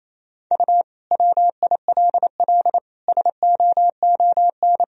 Morse code, heard as a keyed tone, about 28 wpm, 700 Hz. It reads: U WILL SOON
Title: Sentences from Top 400 Words and Abbreviations 28wpm